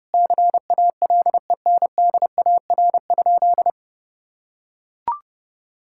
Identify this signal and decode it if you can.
Morse code, heard as a keyed tone, about 32 wpm, 700 Hz. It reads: CALENDAR?
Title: Sentences from Top 1600 Words 30wpm